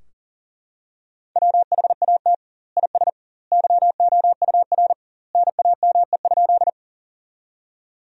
Morse code, heard as a keyed tone, about 40 wpm, 700 Hz. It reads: WHAT IS YOUR NAME?
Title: 2x Word-Spacing - Sentences from Top 200 Words - 40wpm